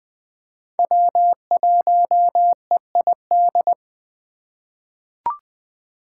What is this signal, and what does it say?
Morse code, heard as a keyed tone, about 20 wpm, 700 Hz. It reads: W1EID
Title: USA Call Signs - 5 Characters - 3x Character Spacing - 20wpm